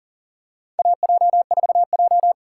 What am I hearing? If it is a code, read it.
Morse code, heard as a keyed tone, about 40 wpm, 700 Hz. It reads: AJ4J
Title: USA Call Signs - 4 Characters - 3x Character Spacing - 40wpm